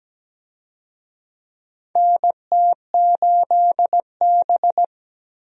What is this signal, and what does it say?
Morse code, heard as a keyed tone, about 17 wpm, 700 Hz. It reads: NT8B
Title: USA Call Signs - 4 Characters - 17wpm